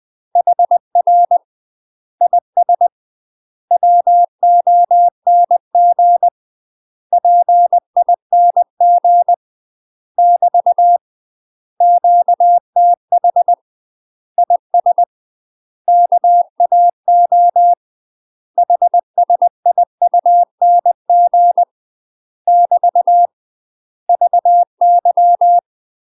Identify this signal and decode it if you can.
Morse code, heard as a keyed tone, about 20 wpm, 700 Hz. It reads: HR IS WONG PING <BT> QTH IS KAO HSIUNG <BT> VY
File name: UK MASTER QSOs TRAINING / 20WPM / UK MASTER QSOs 0012